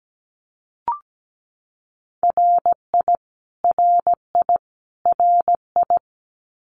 Morse code, heard as a keyed tone, about 17 wpm, 700 Hz. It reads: RI RI RI
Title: US States Repeated 3x 17wpm